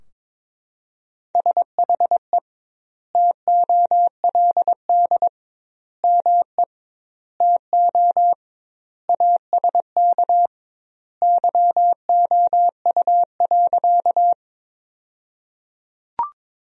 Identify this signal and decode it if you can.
Morse code, heard as a keyed tone, about 22 wpm, 700 Hz. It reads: SHE TOLD ME TO ASK YOU.
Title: 2x Word-Spacing - Sentences from Top 400 Words - 22wpm